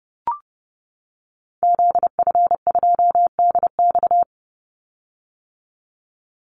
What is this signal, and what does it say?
Morse code, heard as a keyed tone, about 30 wpm, 700 Hz. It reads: ZF2DX
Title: Call Signs - 30wpm